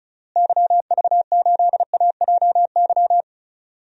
Morse code, heard as a keyed tone, about 36 wpm, 700 Hz. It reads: YV8AJY
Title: DXCC-100 Call Signs - 6 Characters - Repeated 3x - 35wpm